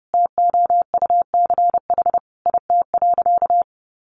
Morse code, heard as a keyed tone, about 30 wpm, 700 Hz. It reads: TOUCH IT.
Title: Sentences from Top 1200 Words 30wpm